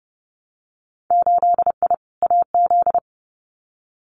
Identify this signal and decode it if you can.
Morse code, heard as a keyed tone, about 30 wpm, 700 Hz. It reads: 8I AZ
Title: Winter Field Day Exchanges Encore 30wpm